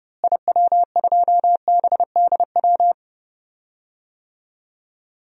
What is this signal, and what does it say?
Morse code, heard as a keyed tone, about 30 wpm, 700 Hz. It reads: IW2BDW
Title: DXCC-100 Call Signs - 6 Characters - Repeated 3x - 30wpm